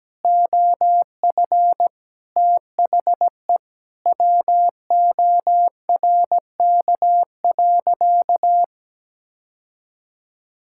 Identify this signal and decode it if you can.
Morse code, heard as a keyed tone, about 17 wpm, 700 Hz. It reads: OF THE WORK.
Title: Sentences from Top 1000 Words - Review Part 6 - 17wpm